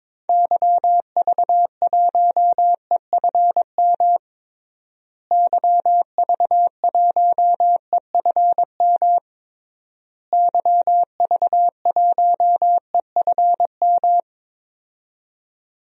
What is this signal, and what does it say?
Morse code, heard as a keyed tone, about 22 wpm, 700 Hz. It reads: YV1EFM YV1EFM YV1EFM
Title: DXCC-100 Call Signs - 6 Characters - Repeated 3x - 22wpm